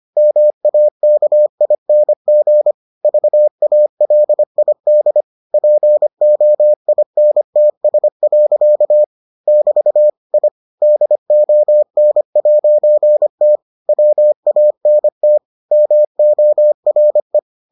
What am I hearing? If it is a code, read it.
Morse code, heard as a keyed tone, about 25 wpm, 600 Hz. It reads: MAKING VALID POINTS. <BT> I DON'T WANT MORE